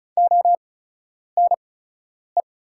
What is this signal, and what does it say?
Morse code, heard as a keyed tone, about 36 wpm, spaced as for 10 wpm, 700 Hz. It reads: ONE
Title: 8x Character Spacing - Top 200 Words - 35wpm